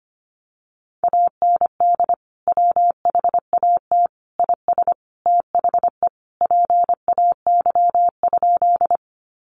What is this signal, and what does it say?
Morse code, heard as a keyed tone, about 25 wpm, 700 Hz. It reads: AND WHAT IS THE PAY?
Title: Sentences from Top 700 Words 25wpm